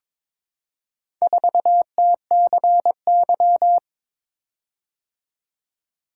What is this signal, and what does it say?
Morse code, heard as a keyed tone, about 22 wpm, 700 Hz. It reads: 4TCY
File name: practice / 22wpm / Four Letter-Number Combinations - 22wpm